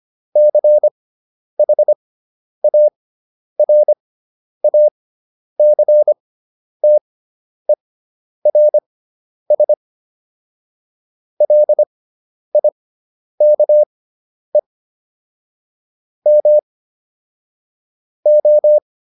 Morse code, heard as a keyed tone, about 25 wpm, spaced as for 10 wpm, 600 Hz. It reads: CHARACTERS LIKE M O